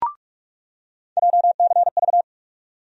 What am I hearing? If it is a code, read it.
Morse code, heard as a keyed tone, about 48 wpm, 700 Hz. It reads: JCV
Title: USA Call Sign Postfixes - 3 Letters 45wpm